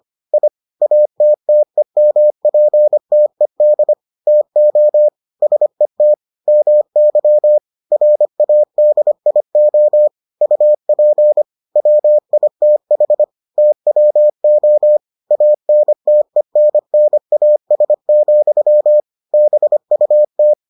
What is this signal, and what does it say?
Morse code, heard as a keyed tone, about 25 wpm, 600 Hz. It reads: I ATTEMPTED TO SET MY RADIO UP WITH TWO ANTENNAS, BUT